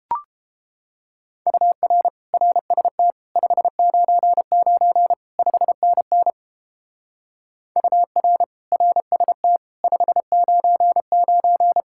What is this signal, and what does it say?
Morse code, heard as a keyed tone, about 29 wpm, 700 Hz. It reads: UR RST 599 5NN UR RST 599